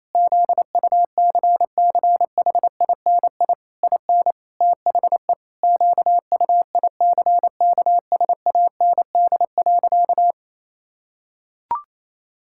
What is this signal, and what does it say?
Morse code, heard as a keyed tone, about 28 wpm, 700 Hz. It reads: ZUCCHINI IN THE QUICKSAND.
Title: Pangram Challenge v2 - 28wpm